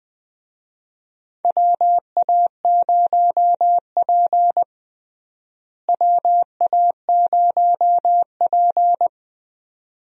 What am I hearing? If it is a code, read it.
Morse code, heard as a keyed tone, about 20 wpm, 700 Hz. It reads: WA0P WA0P
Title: USA Call Signs - 4 Characters - Repeated 2x - 20wpm